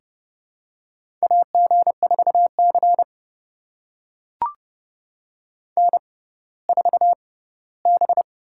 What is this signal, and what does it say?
Morse code, heard as a keyed tone, about 30 wpm, spaced as for 17 wpm, 700 Hz. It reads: AG4C N4B
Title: USA Call Signs - 4 Characters - 6x Character Spacing - 30wpm